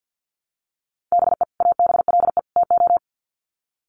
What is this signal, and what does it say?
Morse code, heard as a keyed tone, about 48 wpm, 700 Hz. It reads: BE ABLE TO
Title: Sets of 3 Words 50wpm